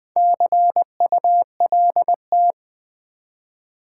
Morse code, heard as a keyed tone, about 20 wpm, 700 Hz. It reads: CULT
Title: Build A Word - 1 Letter At A Time - 4-Letter Words - 20wpm